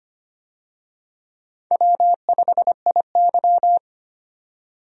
Morse code, heard as a keyed tone, about 25 wpm, 700 Hz. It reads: W5IY
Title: USA Call Signs - 4 Characters - 25wpm